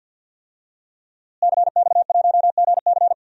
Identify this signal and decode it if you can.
Morse code, heard as a keyed tone, about 52 wpm, 700 Hz. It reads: CX1GC
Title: DXCC-100 Call Signs - 5 Characters - Repeated 3x - 50wpm